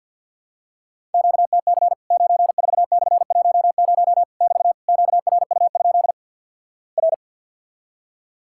Morse code, heard as a keyed tone, about 50 wpm, 700 Hz. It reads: QTC 94/10 <BT> QRU? R
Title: Worked All Europe DX Contest - QTC Exchange - 50wpm